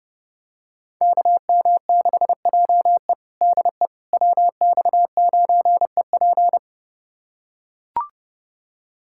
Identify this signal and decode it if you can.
Morse code, heard as a keyed tone, about 32 wpm, 700 Hz. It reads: KM6JE DE WX9EP
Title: USA Callsign DE Callsign 30wpm